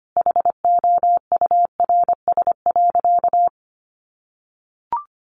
Morse code, heard as a keyed tone, about 25 wpm, 700 Hz. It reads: HOURS.
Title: Additional Easy Phrases 25wpm